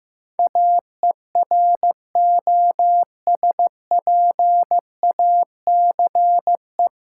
Morse code, heard as a keyed tone, about 16 wpm, 700 Hz. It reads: AEROSPACE
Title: K1USN SST QSO Practice 15wpm